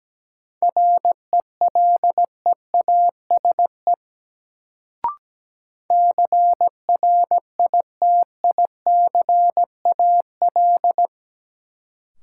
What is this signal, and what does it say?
Morse code, heard as a keyed tone, about 17 wpm, 700 Hz. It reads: RELEASE CRITICAL
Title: Sentences from Top 1100 Words - New Word Drill Alt - 17wpm